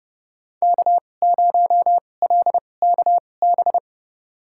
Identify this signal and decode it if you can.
Morse code, heard as a keyed tone, about 32 wpm, spaced as for 22 wpm, 700 Hz. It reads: K0LKB
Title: USA Call Signs - 2x Character Spacing - 30wpm